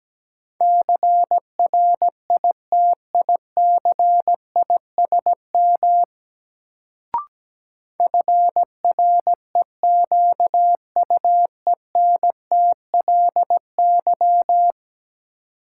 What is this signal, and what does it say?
Morse code, heard as a keyed tone, about 17 wpm, 700 Hz. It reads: CRITICISM FREQUENTLY
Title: Sentences from Top 1300 Words - New Word Drill - 17wpm